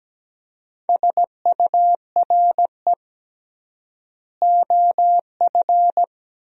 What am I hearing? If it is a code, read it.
Morse code, heard as a keyed tone, about 17 wpm, 700 Hz. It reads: SURE OF
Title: 3x Word-Spacing - Sentences from Top 300 Words - 17wpm